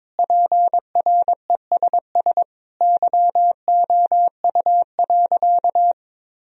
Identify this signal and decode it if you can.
Morse code, heard as a keyed tone, about 22 wpm, 700 Hz. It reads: PRESS YOU.
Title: Sentences from Top 1000 Words - Review Part 1 - 22wpm